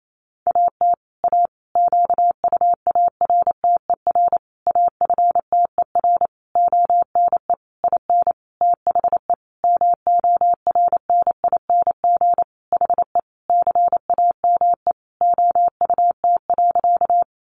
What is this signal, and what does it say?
Morse code, heard as a keyed tone, about 28 wpm, 700 Hz. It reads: AT A QUARTER AFTER ONE IN THE MORNING HE CAME OUT.
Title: Sentences from Top 1300 Words 28wpm